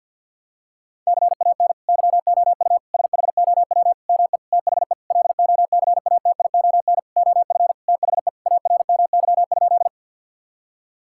Morse code, heard as a keyed tone, about 51 wpm, 700 Hz. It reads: CAN YOU SHOW ME THE LOCATION OF THE ARMY?
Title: Sentences from Top 1000 Words - Review Part 6 - 50wpm